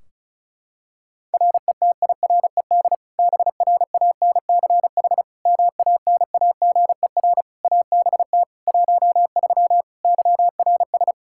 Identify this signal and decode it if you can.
Morse code, heard as a keyed tone, about 36 wpm, 700 Hz. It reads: RETIRED BRANCH MANAGER ABT 13 YRS
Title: K1USN SST QSO Practice 35wpm